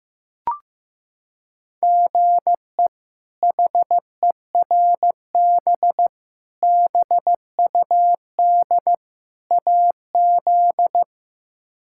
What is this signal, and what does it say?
Morse code, heard as a keyed tone, about 15 wpm, 700 Hz. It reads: GE HERB BUD AZ
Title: K1USN SST QSO Practice - 15wpm